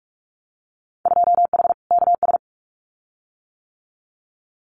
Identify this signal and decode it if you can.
Morse code, heard as a keyed tone, about 46 wpm, 700 Hz. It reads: 2H KS
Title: Winter Field Day Exchanges Encore 45wpm